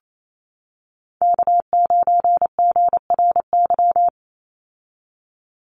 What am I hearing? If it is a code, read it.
Morse code, heard as a keyed tone, about 28 wpm, 700 Hz. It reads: K9GRY
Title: USA Call Signs 28wpm